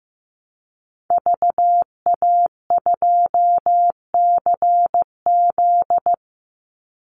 Morse code, heard as a keyed tone, about 15 wpm, 700 Hz. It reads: VA2CZ
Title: Call Signs Repeated 3x - 15wpm